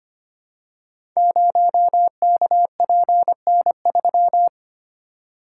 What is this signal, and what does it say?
Morse code, heard as a keyed tone, about 25 wpm, 700 Hz. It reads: 0KPN3
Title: Code Groups - 5 Letter:Numbers - 25wpm